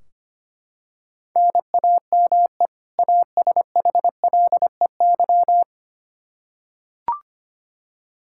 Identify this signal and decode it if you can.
Morse code, heard as a keyed tone, about 25 wpm, 700 Hz. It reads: NAME ASHLEY
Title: Practice QSO - Exchange 1 - Beginning - 25wpm